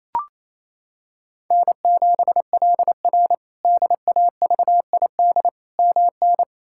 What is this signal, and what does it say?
Morse code, heard as a keyed tone, about 28 wpm, 700 Hz. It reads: N7LR DAVID MN
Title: K1USN SST QSO Practice - 28wpm